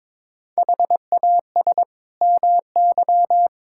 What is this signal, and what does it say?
Morse code, heard as a keyed tone, about 22 wpm, 700 Hz. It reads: HAS MY
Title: Sentences from Top 1600 Words 22wpm